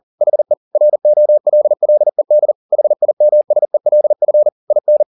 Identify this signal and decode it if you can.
Morse code, heard as a keyed tone, about 40 wpm, 600 Hz. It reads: HE ROLLED HIMSELF IN